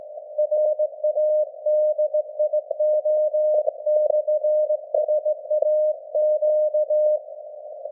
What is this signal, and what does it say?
Morse code, heard as a keyed tone, about 16 wpm, 610 Hz. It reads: RADIO/SAQ
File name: laengstwellenempfang / SDRSharp 17kHz AF